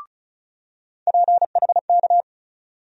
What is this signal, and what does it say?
Morse code, heard as a keyed tone, about 36 wpm, 700 Hz. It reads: PHK
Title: USA Call Sign Postfixes - 3 Letters 35wpm